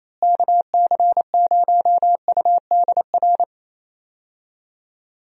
Morse code, heard as a keyed tone, about 27 wpm, 700 Hz. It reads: KC0UDR
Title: USA Call Signs - 6 Characters - Repeated 2x - 28wpm